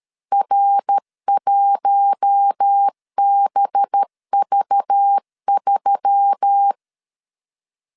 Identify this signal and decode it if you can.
Morse code, heard as a keyed tone, about 13 wpm, 800 Hz. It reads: R1BV3
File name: CW / DET / 1 6 1500C S A 12wpm 3440